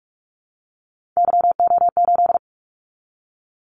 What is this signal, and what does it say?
Morse code, heard as a keyed tone, about 46 wpm, 700 Hz. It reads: YO8
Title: Callsign Prefixes for Top 25 DXCC - 45wpm